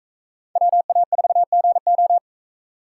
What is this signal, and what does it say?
Morse code, heard as a keyed tone, about 44 wpm, 700 Hz. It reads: WA4GO